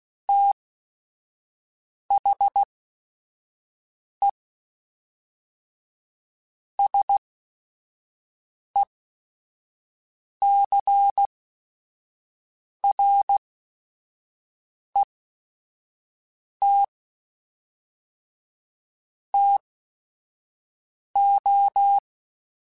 Morse code, heard as a keyed tone, about 16 wpm, spaced as for 5 wpm, 800 Hz. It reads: THE SECRET TO
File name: Technical Subjects / Technical Subjects-mp3s / Sw Pwr Sup @5